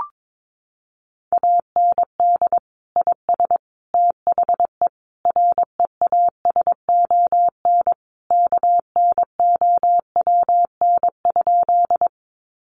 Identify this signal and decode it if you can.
Morse code, heard as a keyed tone, about 22 wpm, 700 Hz. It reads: AND IS THE REASON KNOWN?
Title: Sentences from Top 700 Words 22wpm